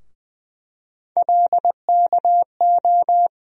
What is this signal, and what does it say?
Morse code, heard as a keyed tone, about 20 wpm, 700 Hz. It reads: LKO
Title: USA Call Sign Postfixes - 3 Letters 20wpm